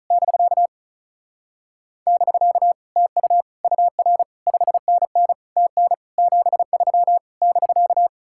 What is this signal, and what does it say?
Morse code, heard as a keyed tone, about 44 wpm, 700 Hz. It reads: <BK> <BK> TU UR 5NN TN 73 <BK>